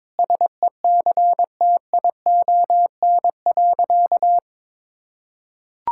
Morse code, heard as a keyed tone, about 22 wpm, 700 Hz. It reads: SECTION.
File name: Sentences with 9-Word Phrases - 22wpm